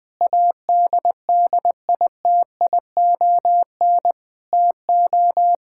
Morse code, heard as a keyed tone, about 21 wpm, 700 Hz. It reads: ADDITION TO
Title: Sentences from Top 1000 Words - Review Part 6 - 20wpm